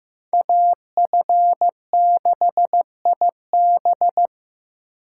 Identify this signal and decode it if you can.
Morse code, heard as a keyed tone, about 18 wpm, 700 Hz. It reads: AF6IB
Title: Speed Racing - USA 2x2 Call Signs 15wpm